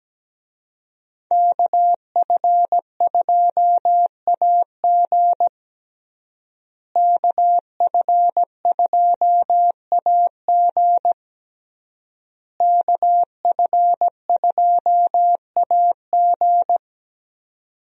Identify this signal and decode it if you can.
Morse code, heard as a keyed tone, about 17 wpm, 700 Hz. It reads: KF2AG KF2AG KF2AG
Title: USA Call Signs - 5 Characters - Repeated 3x - 17wpm